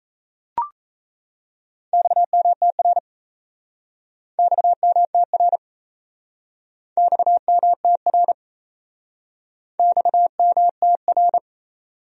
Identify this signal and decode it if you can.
Morse code, heard as a keyed tone, about 37 wpm, 700 Hz. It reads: XMTR XMTR XMTR XMTR